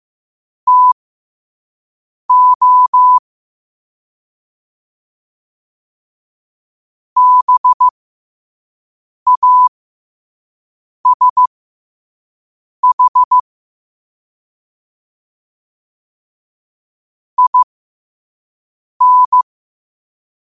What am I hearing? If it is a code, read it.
Morse code, heard as a keyed tone, about 15 wpm, spaced as for 5 wpm, 1000 Hz. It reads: TO BASH IN
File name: qotdfiles / 5wpm